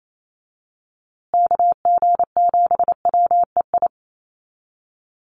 Morse code, heard as a keyed tone, about 28 wpm, 700 Hz. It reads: KG7WEI
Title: USA Call Signs Repeated 2x 28wpm